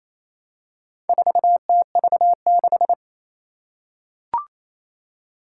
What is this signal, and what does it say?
Morse code, heard as a keyed tone, about 27 wpm, 700 Hz. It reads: 4TV6
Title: Four Challenging Letter-Number Combinations 28wpm